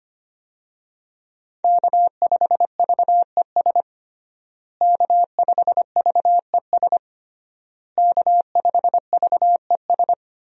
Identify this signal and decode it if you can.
Morse code, heard as a keyed tone, about 25 wpm, 700 Hz. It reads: K5VES K5VES K5VES
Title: USA Call Signs - 5 Characters - Repeated 3x - 25wpm